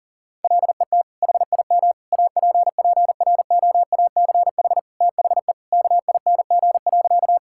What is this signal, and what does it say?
Morse code, heard as a keyed tone, about 37 wpm, 700 Hz. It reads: LET HIM APPROACH THE KING.
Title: Sentences from Top 1000 Words - Review Part 6 - 40wpm